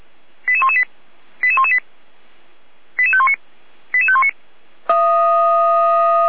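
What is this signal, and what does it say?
Five-tone selective call (ZVEI), heard as quick runs of five tones, tones of about 70 ms. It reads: 88118 88118 88519 88519
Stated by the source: Five-tone ZVEI selective call received by SDR, address 88491.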